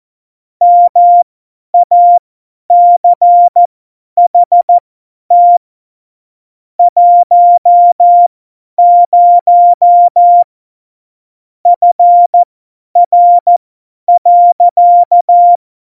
Morse code, heard as a keyed tone, about 14 wpm, spaced as for 10 wpm, 700 Hz. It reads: MACHT 10 FR.
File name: Morsekurs / Witze / Witz 05. Februar 2019, 50 F70